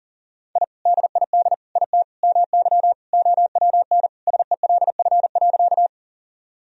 Morse code, heard as a keyed tone, about 40 wpm, 700 Hz. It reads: I DID IT MY OWN SELF.
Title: Sentences from Top 1000 Words - Review Part 1 - 40wpm